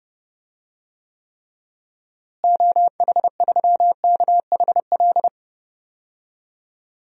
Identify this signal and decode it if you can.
Morse code, heard as a keyed tone, about 32 wpm, 700 Hz. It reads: OH3KHL
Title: DXCC-100 Call Signs - 6 Characters - Repeated 3x - 30wpm